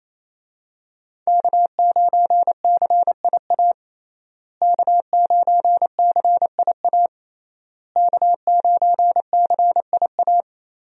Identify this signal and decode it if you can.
Morse code, heard as a keyed tone, about 28 wpm, 700 Hz. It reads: K9CIA K9CIA K9CIA
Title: USA Call Signs - 5 Characters - Repeated 3x - 28wpm